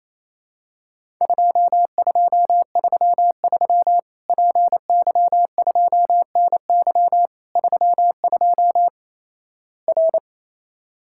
Morse code, heard as a keyed tone, about 28 wpm, 700 Hz. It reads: 2233 PY2NY 32 R
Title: Worked All Europe DX Contest - QTC Exchange - 28wpm